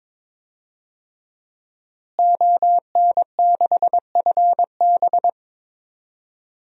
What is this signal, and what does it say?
Morse code, heard as a keyed tone, about 22 wpm, 700 Hz. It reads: ON6FB
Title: DXCC-100 Call Signs - 5 Characters - Repeated 3x - 22wpm